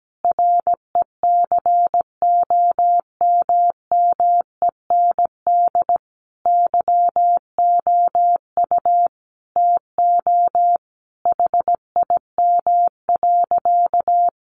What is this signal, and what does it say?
Morse code, heard as a keyed tone, about 17 wpm, 700 Hz. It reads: RECOMMEND YOU TO HIM.
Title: Sentences from Top 1200 Words 17wpm